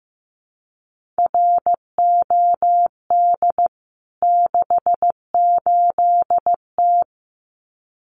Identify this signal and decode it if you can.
Morse code, heard as a keyed tone, about 15 wpm, 700 Hz. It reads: ROD 68T
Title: CWT Exchange 15wpm